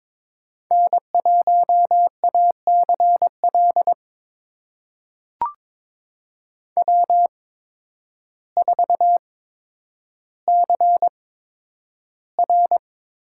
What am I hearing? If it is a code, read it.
Morse code, heard as a keyed tone, about 21 wpm, spaced as for 10 wpm, 700 Hz. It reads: N1ACL W4CR
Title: USA Call Signs - 8x Character Spacing - 22wpm